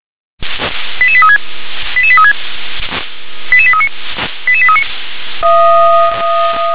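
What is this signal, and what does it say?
Five-tone selective call (ZVEI), heard as quick runs of five tones, tones of about 70 ms. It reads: 88836 88836 88839 88839
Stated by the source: ZVEI five-tone call 88166 received by SDR.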